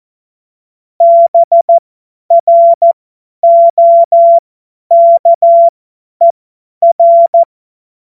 Morse code, heard as a keyed tone, about 14 wpm, spaced as for 10 wpm, 680 Hz. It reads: BROKER
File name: Morsekurs / Witze / Witz 10 Juni 2025 50 F70